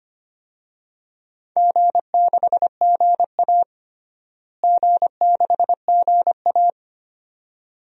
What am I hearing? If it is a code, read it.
Morse code, heard as a keyed tone, about 25 wpm, 700 Hz. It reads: G6GA G6GA
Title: DXCC-100 Call Signs - 4 Characters - Repeated 2x - 25wpm